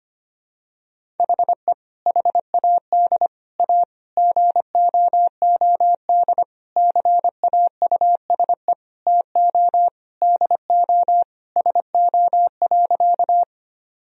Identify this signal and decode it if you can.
Morse code, heard as a keyed tone, about 25 wpm, 700 Hz. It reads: HE HAD A GOOD CAUSE TO DO SO.
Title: Sentences from Top 1000 Words - Review Part 1 - 25wpm